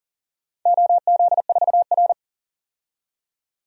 Morse code, heard as a keyed tone, about 40 wpm, 700 Hz. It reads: OZ4R
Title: DXCC-100 Call Signs - 4 Characters - 40wpm